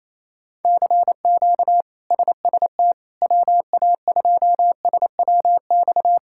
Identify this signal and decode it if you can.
Morse code, heard as a keyed tone, about 28 wpm, 700 Hz. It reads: CQ SST WA2SWX
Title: K1USN SST QSO Practice - 28wpm